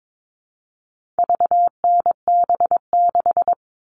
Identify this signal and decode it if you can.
Morse code, heard as a keyed tone, about 22 wpm, 700 Hz. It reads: VNB6
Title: All Cut Numbers - 4-Digit 22wpm